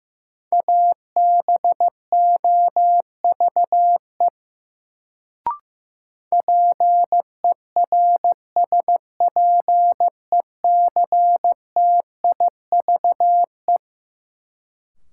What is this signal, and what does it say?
Morse code, heard as a keyed tone, about 15 wpm, 700 Hz. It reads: ABOVE PERSPECTIVE
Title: Sentences from Top 1200 Words - New Word Drill Alt - 15wpm